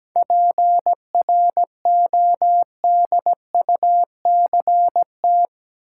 Morse code, heard as a keyed tone, about 17 wpm, 700 Hz. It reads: PRODUCT
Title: Sentences from Top 1400 Words - New Word Drill Alt - 17wpm